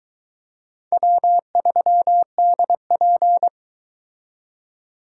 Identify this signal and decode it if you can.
Morse code, heard as a keyed tone, about 22 wpm, 700 Hz. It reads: W3DP